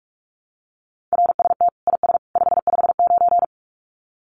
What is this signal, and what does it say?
Morse code, heard as a keyed tone, about 44 wpm, 700 Hz. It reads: RST IS 559
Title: Common QSO Phrases 45wpm